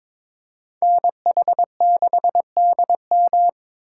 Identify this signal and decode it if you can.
Morse code, heard as a keyed tone, about 26 wpm, 700 Hz. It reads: NH6DM
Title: Speed Racing - USA 2x2 Call Signs 22wpm